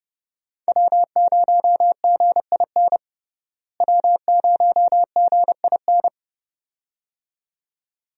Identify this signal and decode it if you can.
Morse code, heard as a keyed tone, about 30 wpm, 700 Hz. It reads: W0GIN W0GIN
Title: USA Call Signs - 5 Characters - Repeated 2x - 30wpm